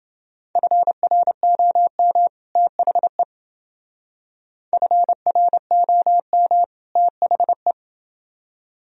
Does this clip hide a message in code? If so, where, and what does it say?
Yes: Morse code from 0.6 to 7.7 s, heard as a keyed tone, about 28 wpm, 700 Hz. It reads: FROM THE FROM THE